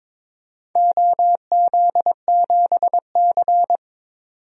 Morse code, heard as a keyed tone, about 22 wpm, 700 Hz. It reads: OZ7C
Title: DXCC-100 Call Signs - 4 Characters - 22wpm